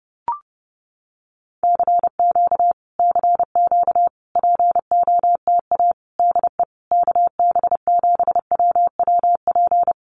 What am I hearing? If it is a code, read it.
Morse code, heard as a keyed tone, about 30 wpm, 700 Hz. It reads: CQ CQ POTA DE KB7WWP
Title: Common QSO Phrases Encore 30wpm